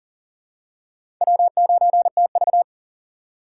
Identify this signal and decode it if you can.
Morse code, heard as a keyed tone, about 40 wpm, 700 Hz. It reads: W9TV
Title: USA Call Signs - 4 Characters - 2x Character Spacing - 40wpm